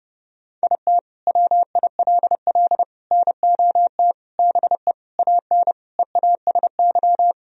Morse code, heard as a keyed tone, about 30 wpm, 700 Hz. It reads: IT WILL NOT BE AN EASY
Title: Sentences from Top 1000 Words - Review Part 2 - 30wpm